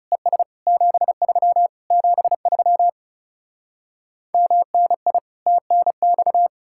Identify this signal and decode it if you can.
Morse code, heard as a keyed tone, about 40 wpm, 700 Hz. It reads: ES 73 73 MNI TNX